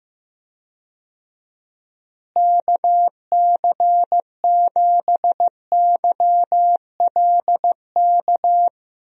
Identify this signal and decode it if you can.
Morse code, heard as a keyed tone, about 15 wpm, 700 Hz. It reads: KC7YLK
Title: USA Call Signs - 6 Characters - Repeated 2x - 15wpm